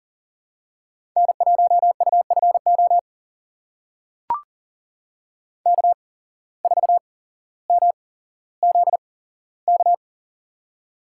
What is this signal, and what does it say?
Morse code, heard as a keyed tone, about 40 wpm, spaced as for 17 wpm, 700 Hz. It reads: N1UFO K4MZK
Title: USA Call Signs - 5 Characters - 8x Character Spacing - 40wpm